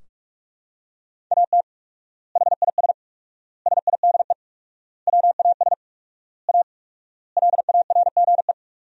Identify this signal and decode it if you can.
Morse code, heard as a keyed tone, about 46 wpm, 700 Hz. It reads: AT HIS SIDE WAS A LARGE
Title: 4x Word-Spacing - Sentences from Top 500 Words - 45wpm